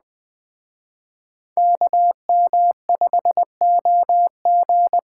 Morse code, heard as a keyed tone, about 21 wpm, 700 Hz. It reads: KM5OG